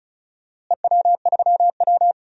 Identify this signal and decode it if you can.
Morse code, heard as a keyed tone, about 34 wpm, 700 Hz. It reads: EW3W
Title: DXCC-100 Call Signs - 4 Characters - Repeated 2x - 35wpm